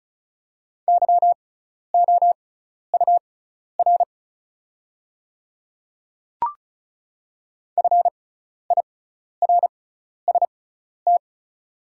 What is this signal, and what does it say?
Morse code, heard as a keyed tone, about 36 wpm, spaced as for 12 wpm, 700 Hz. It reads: YOUR FIRST
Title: 6x Character Spacing - Top 100 Words - 35wpm